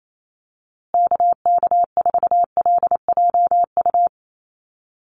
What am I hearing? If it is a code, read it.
Morse code, heard as a keyed tone, about 28 wpm, 700 Hz. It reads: KK4LJU
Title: USA Call Signs Repeated 2x 28wpm